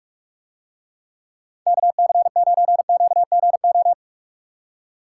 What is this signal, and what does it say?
Morse code, heard as a keyed tone, about 49 wpm, 700 Hz. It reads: KC9QGO